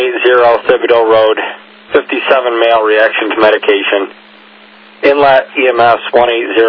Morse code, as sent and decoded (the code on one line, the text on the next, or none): none